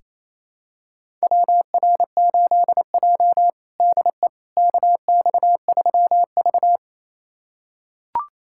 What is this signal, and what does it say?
Morse code, heard as a keyed tone, about 28 wpm, 700 Hz. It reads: WR8J DE KX3V
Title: USA Callsign DE Callsign - 4 Characters - 28wpm